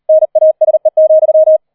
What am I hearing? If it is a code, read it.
Morse code, heard as a keyed tone, about 38 wpm, 630 Hz. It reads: NASE,
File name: 004 Vierter Streich 40wpm